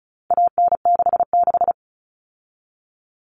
Morse code, heard as a keyed tone, about 34 wpm, 700 Hz. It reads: AN66
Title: All Cut Numbers - 4-Digit 35wpm